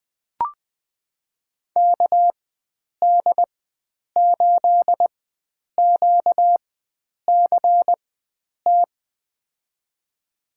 Morse code, heard as a keyed tone, about 20 wpm, spaced as for 9 wpm, 700 Hz. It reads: KD8QCT
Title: USA Call Signs - 4x Character Spacing - 20wpm